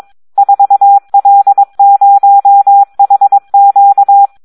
Morse code, heard as a keyed tone, about 22 wpm, 800 Hz. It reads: 4L0HQ
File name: Audio practica CW / SD 1 22